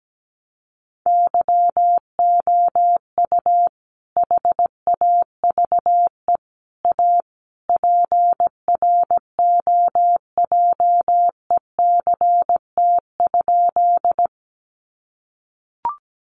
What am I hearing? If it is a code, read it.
Morse code, heard as a keyed tone, about 17 wpm, 700 Hz. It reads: YOU HAVE A PROJECT?
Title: Sentences from Top 1400 Words 17wpm